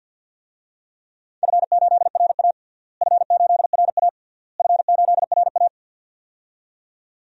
Morse code, heard as a keyed tone, about 52 wpm, 700 Hz. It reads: F8RA F8RA F8RA
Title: DXCC-100 Call Signs - 4 Characters - Repeated 3x - 50wpm